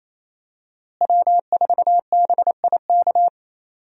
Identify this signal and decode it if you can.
Morse code, heard as a keyed tone, about 28 wpm, 700 Hz. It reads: W4BIK
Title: USA Call Signs - 5 Characters - Repeated 2x - 28wpm